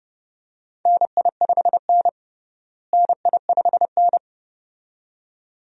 Morse code, heard as a keyed tone, about 30 wpm, 700 Hz. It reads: NI5N NI5N
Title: USA Call Signs - 4 Characters - Repeated 2x - 30wpm